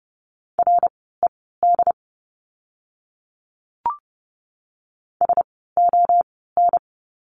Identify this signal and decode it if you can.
Morse code, heard as a keyed tone, about 30 wpm, spaced as for 17 wpm, 700 Hz. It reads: RED SON
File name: Lesson 16f - Letter c - Words 30wpm